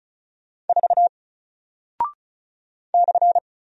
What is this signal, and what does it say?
Morse code, heard as a keyed tone, about 36 wpm, 700 Hz. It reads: 4 /
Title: Single Character (Letter:Number:Punctuation:Prosign) Alt 35wpm